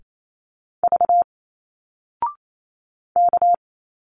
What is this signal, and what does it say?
Morse code, heard as a keyed tone, about 28 wpm, 700 Hz. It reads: V K
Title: Single Letters 28wpm